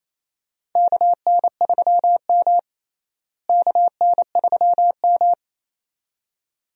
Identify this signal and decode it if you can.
Morse code, heard as a keyed tone, about 28 wpm, 700 Hz. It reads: KN3M KN3M
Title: USA Call Signs - 4 Characters - Repeated 2x - 28wpm